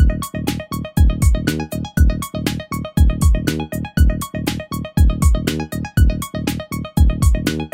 There is no code.